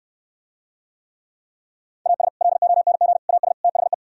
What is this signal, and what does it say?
Morse code, heard as a keyed tone, about 67 wpm, 700 Hz. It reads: AS LONG AS THE